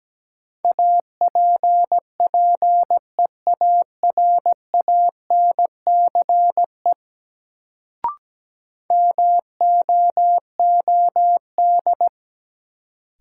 Morse code, heard as a keyed tone, about 17 wpm, 700 Hz. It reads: APPEARANCE MOOD
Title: Sentences from Top 1100 Words - New Word Drill - 17wpm